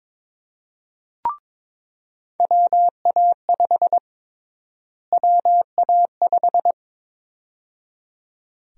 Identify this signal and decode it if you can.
Morse code, heard as a keyed tone, about 22 wpm, 700 Hz. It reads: WA5 WA5
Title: USA Call Sign Prefixes - Repeated 2x - 22wpm